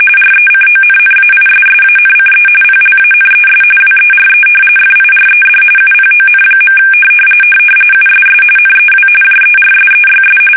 RTTY, about 45.45 baud. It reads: RECONNAISSANCE PLANES WERE FIRED UPON. USS NEW JERSEYS
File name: fsk850 / battleshipnj